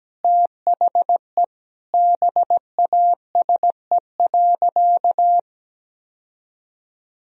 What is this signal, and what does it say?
Morse code, heard as a keyed tone, about 17 wpm, 700 Hz. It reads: THE BASE.
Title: Sentences from Top 1000 Words - Review Part 1 - 17wpm